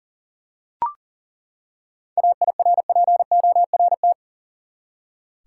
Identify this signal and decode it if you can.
Morse code, heard as a keyed tone, about 40 wpm, 700 Hz. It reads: AIRPORT
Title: Seven Letter Words - 40wpm